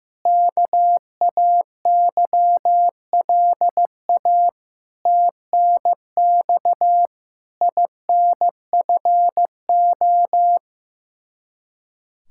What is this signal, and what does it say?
Morse code, heard as a keyed tone, about 15 wpm, 700 Hz. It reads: KAYLA TNX INFO
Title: K1USN SST QSO Practice 15wpm